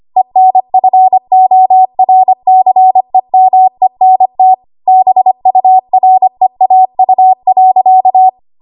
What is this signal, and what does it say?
Morse code, heard as a keyed tone, about 25 wpm, 750 Hz. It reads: ENFORCEMENT BUREAU.
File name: Archive / 25 WPM / 25WPM